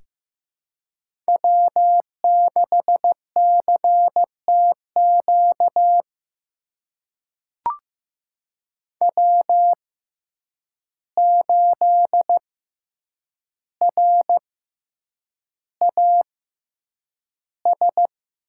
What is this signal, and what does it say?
Morse code, heard as a keyed tone, about 15 wpm, spaced as for 8 wpm, 700 Hz. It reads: W6CTQ W8RAS
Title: USA Call Signs - 5 Characters - 6x Character Spacing - 15wpm